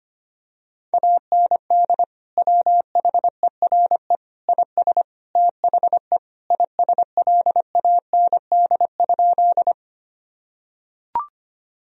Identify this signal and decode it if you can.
Morse code, heard as a keyed tone, about 25 wpm, 700 Hz. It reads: AND WHERE IS THE ISLAND?
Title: Sentences from Top 1000 Words - Review Part 3 - 25wpm